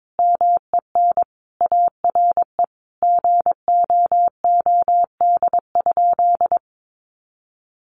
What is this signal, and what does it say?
Morse code, heard as a keyed tone, about 22 wpm, 700 Hz. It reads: MEN ARE GOOD?
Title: Sentences from Top 1300 Words 22wpm